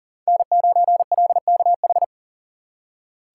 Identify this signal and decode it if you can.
Morse code, heard as a keyed tone, about 40 wpm, 700 Hz. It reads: N9LKH
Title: USA Call Signs - 5 Characters - 2x Character Spacing - 40wpm